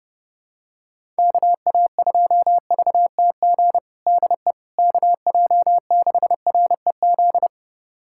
Morse code, heard as a keyed tone, about 32 wpm, 700 Hz. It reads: KA2VTG DE KJ6REZ
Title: USA Callsign DE Callsign - 6 Characters - 30wpm